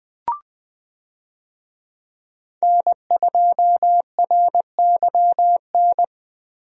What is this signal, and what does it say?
Morse code, heard as a keyed tone, about 19 wpm, 700 Hz. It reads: N2RYN
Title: USA Call Signs - 5 Characters - Repeated 3x - 20wpm